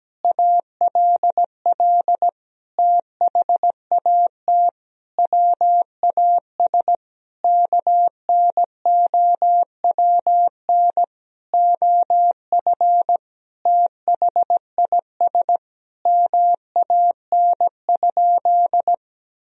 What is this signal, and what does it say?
Morse code, heard as a keyed tone, about 17 wpm, 700 Hz. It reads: ALL THAT WAS KNOWN OF THIS MAN?
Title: Sentences from Top 1000 Words - Review Part 3 - 17wpm